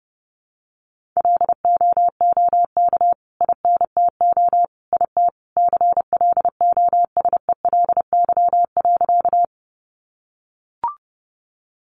Morse code, heard as a keyed tone, about 28 wpm, 700 Hz. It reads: LOOK INTO IT CLOSELY.
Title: Sentences from Top 1400 Words 30wpm